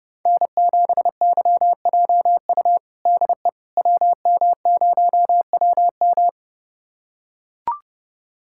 Morse code, heard as a keyed tone, about 32 wpm, 700 Hz. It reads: N7YJU DE WM0WM
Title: USA Callsign DE Callsign - 5 Characters - 30wpm